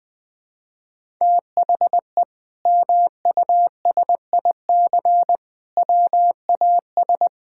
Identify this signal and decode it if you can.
Morse code, heard as a keyed tone, about 20 wpm, 700 Hz. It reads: THE MUSIC WAS
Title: Sentences from Top 1000 Words - Review Part 6 - 20wpm